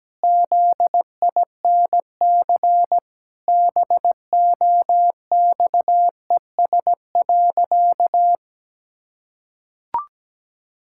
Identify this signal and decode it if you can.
Morse code, heard as a keyed tone, about 17 wpm, 700 Hz. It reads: ZINC BOXES.
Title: Pangram Challenge v2 - 17wpm